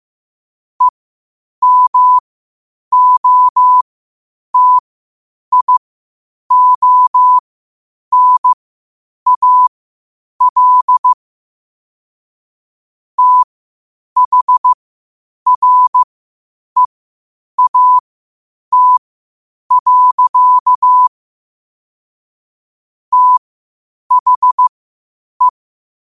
Morse code, heard as a keyed tone, about 15 wpm, spaced as for 8 wpm, 1000 Hz. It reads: EMOTIONAL THREAT. THE
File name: qotdfiles / 8wpm